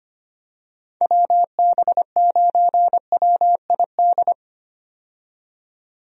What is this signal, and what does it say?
Morse code, heard as a keyed tone, about 25 wpm, 700 Hz. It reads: WB9WID
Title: USA Call Signs - 6 Characters - 25wpm